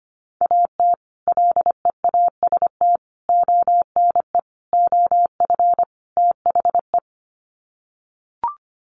Morse code, heard as a keyed tone, about 25 wpm, 700 Hz. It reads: AT LEAST ONE OF THE
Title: Sets of 5 Words Encore 25wpm